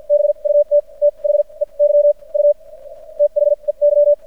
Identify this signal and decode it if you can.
Morse code, heard as a keyed tone, about 56 wpm, spaced as for 39 wpm, 590 Hz. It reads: BUT THEY U THEY
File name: archived / CRSnet